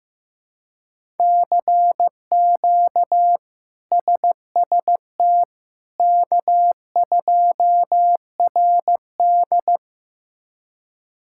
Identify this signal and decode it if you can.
Morse code, heard as a keyed tone, about 15 wpm, 700 Hz. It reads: CQ SST K2RD
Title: K1USN SST QSO Practice - 15wpm